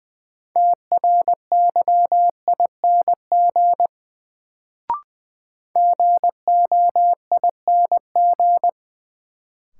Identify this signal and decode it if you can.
Morse code, heard as a keyed tone, about 21 wpm, 700 Hz. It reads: TRYING GOING
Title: Sets of 2 Words - New Word Drill - 20wpm